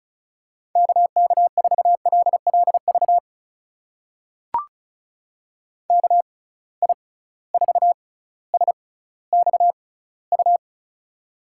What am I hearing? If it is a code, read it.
Morse code, heard as a keyed tone, about 36 wpm, spaced as for 18 wpm, 700 Hz. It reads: KK4LLV KI4SXU
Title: USA Call Signs - 6 Characters - 6x Character Spacing - 35wpm